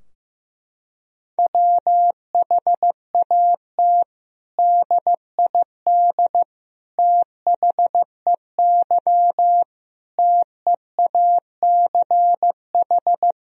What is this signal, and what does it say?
Morse code, heard as a keyed tone, about 15 wpm, 700 Hz. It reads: WHAT DID THEY TEACH
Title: Sentences from Top 1000 Words - Review Part 2 - 15wpm